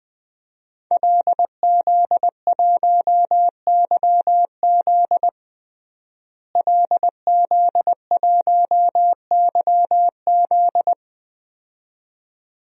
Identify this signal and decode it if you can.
Morse code, heard as a keyed tone, about 21 wpm, 700 Hz. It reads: LZ1YZ LZ1YZ
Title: DXCC-100 Call Signs - 5 Characters - Repeated 2x - 20wpm